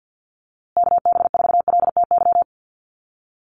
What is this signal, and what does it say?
Morse code, heard as a keyed tone, about 50 wpm, 700 Hz. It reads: KB4LTY
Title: USA Call Signs Repeated 2x 50wpm